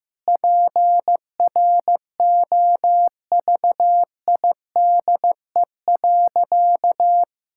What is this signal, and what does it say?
Morse code, heard as a keyed tone, about 15 wpm, 700 Hz. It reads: PROVIDE.
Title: Sentences from Top 1000 Words - Review Part 6 - 15wpm